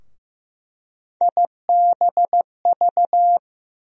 Morse code, heard as a keyed tone, about 15 wpm, 700 Hz. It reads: IBV
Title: USA Call Sign Postfixes - 3 Letters 15wpm